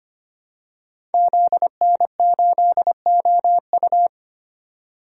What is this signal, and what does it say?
Morse code, heard as a keyed tone, about 25 wpm, 700 Hz. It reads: ZN8OU
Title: Code Groups - 5 Letter:Numbers - 25wpm